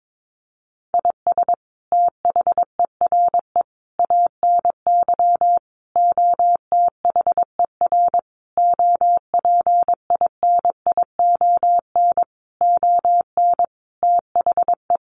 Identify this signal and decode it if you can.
Morse code, heard as a keyed tone, about 22 wpm, 700 Hz. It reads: IS THERE ANY OTHER OPINION ON THE
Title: Sentences from Top 1200 Words 22wpm